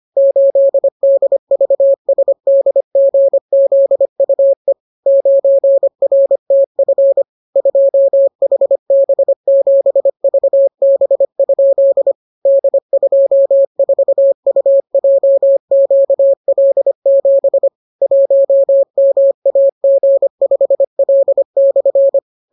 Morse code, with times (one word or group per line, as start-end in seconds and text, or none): 0.2-4.7 8DVSDGZUE
5.1-7.2 9RTF
7.6-12.1 2HB7VB?
12.5-17.7 D24UJQL7
18.0-22.2 1MAG5L/